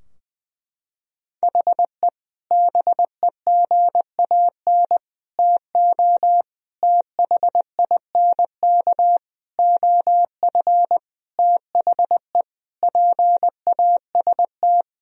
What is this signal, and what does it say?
Morse code, heard as a keyed tone, about 20 wpm, 700 Hz. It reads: HE BEGAN TO THINK OF THE PAST
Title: Sentences from Top 1000 Words - Review Part 3 - 20wpm